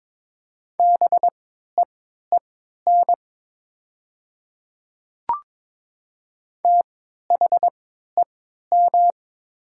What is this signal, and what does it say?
Morse code, heard as a keyed tone, about 22 wpm, spaced as for 12 wpm, 700 Hz. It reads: BEEN THEM
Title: 3x Character Spacing - Top 100 Words - 22wpm